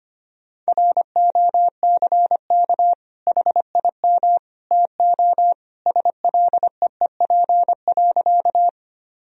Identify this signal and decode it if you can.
Morse code, heard as a keyed tone, about 25 wpm, 700 Hz. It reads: ROCK HIM TO SLEEP.
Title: Sentences from Top 1000 Words - Review Part 2 - 25wpm